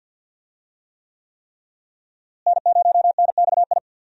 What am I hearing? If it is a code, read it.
Morse code, heard as a keyed tone, about 52 wpm, 690 Hz. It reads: N0NXI
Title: USA Call Signs - 5 Characters - Repeated 3x - 50wpm